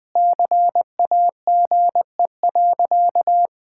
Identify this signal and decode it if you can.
Morse code, heard as a keyed tone, about 20 wpm, 700 Hz. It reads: CAGE.
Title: Easy Sentences with Hints - 20wpm